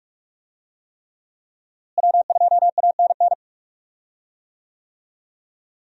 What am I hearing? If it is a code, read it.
Morse code, heard as a keyed tone, about 44 wpm, 700 Hz. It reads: W2ANN
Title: USA Call Signs - 5 Characters - Repeated 3x - 45wpm